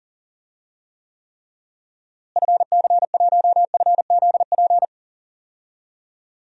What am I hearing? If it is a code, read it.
Morse code, heard as a keyed tone, about 40 wpm, 700 Hz. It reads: FC1FZP
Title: DXCC-100 Call Signs - 6 Characters - Repeated 3x - 40wpm